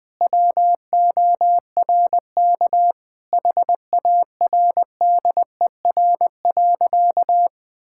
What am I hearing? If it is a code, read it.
Morse code, heard as a keyed tone, about 19 wpm, 700 Hz. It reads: WORK HARDER.
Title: Sentences with 6-Word Phrases - 20wpm